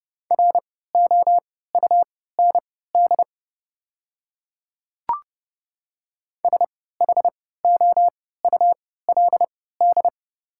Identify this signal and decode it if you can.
Morse code, heard as a keyed tone, about 30 wpm, spaced as for 17 wpm, 700 Hz. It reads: ROUND SHOULD
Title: 3x Character Spacing - Top 300 Words - 30wpm